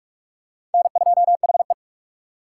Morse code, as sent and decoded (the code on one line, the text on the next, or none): -. ..--- .... .
N2HE